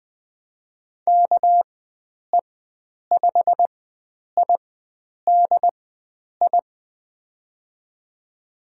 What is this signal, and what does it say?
Morse code, heard as a keyed tone, about 20 wpm, spaced as for 9 wpm, 700 Hz. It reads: KE5IDI
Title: USA Call Signs - 6 Characters - 4x Character Spacing - 20wpm